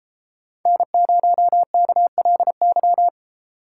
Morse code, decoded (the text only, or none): N0KLY